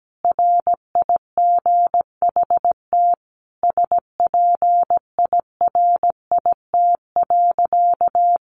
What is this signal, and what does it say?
Morse code, heard as a keyed tone, about 17 wpm, 700 Hz. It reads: RIGHT SPIRIT.
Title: Sentences from Top 1400 Words 17wpm